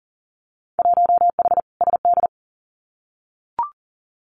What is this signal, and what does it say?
Morse code, heard as a keyed tone, about 40 wpm, 700 Hz. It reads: 1H SD
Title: Winter Field Day Exchanges 40wpm